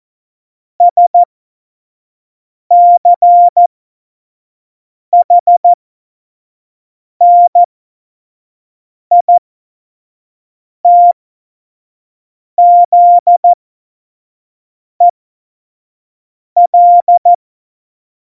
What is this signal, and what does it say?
Morse code, heard as a keyed tone, about 14 wpm, spaced as for 5 wpm, 700 Hz. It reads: SCHNITZEL
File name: Morsekurs / Witze / Witz 29. Oktober 2019, 25 F70